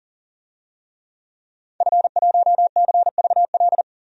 Morse code, heard as a keyed tone, about 40 wpm, 700 Hz. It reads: F1CVL
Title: DXCC-100 Call Signs - 5 Characters - 40wpm